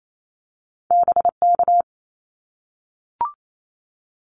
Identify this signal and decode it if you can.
Morse code, heard as a keyed tone, about 28 wpm, 700 Hz. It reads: BK
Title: Abbreviations 28wpm